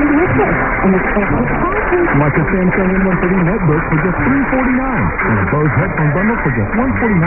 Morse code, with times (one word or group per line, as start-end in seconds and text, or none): none